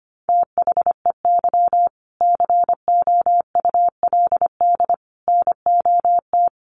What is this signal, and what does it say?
Morse code, heard as a keyed tone, about 25 wpm, 700 Hz. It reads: THEY COULD NOT
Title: Sentences from Top 1100 Words 25wpm